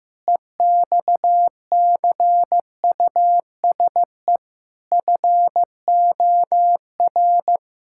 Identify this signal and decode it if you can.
Morse code, heard as a keyed tone, about 16 wpm, 700 Hz. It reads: EXCUSE FOR
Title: Sentences from Top 1000 Words - Review Part 6 - 15wpm